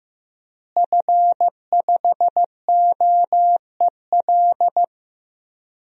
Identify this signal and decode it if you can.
Morse code, heard as a keyed tone, about 15 wpm, 700 Hz. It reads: F5OEL
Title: DXCC-100 Call Signs - 5 Characters - Repeated 2x - 15wpm